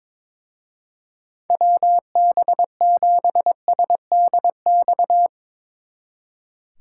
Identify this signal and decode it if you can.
Morse code, heard as a keyed tone, about 22 wpm, 700 Hz. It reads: WB7SDX
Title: Speed Racing - USA 2x3 Call Signs 22wpm